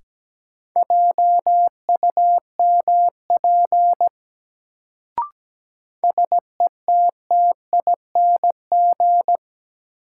Morse code, heard as a keyed tone, about 17 wpm, 700 Hz. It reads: JUMP SETTING
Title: Sentences from Top 1300 Words - New Word Drill - 17wpm